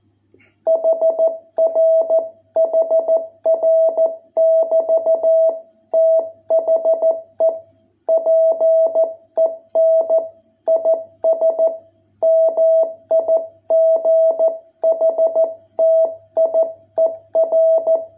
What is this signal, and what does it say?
Morse code, heard as a keyed tone, about 14 wpm, 660 Hz. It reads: HR HR <BT> THE PEN IS MIGHTIER